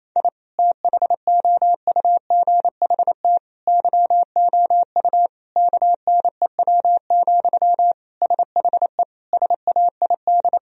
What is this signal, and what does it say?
Morse code, heard as a keyed tone, about 28 wpm, 700 Hz. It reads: I THOUGHT YOU KNEW, SHE SAID
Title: Sentences from Top 1000 Words - Review Part 4 - 28wpm